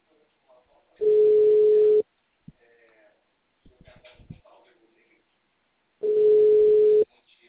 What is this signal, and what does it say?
Ringback tone (Europe, 425 Hz), with speech.
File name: out- -2012- - - .21571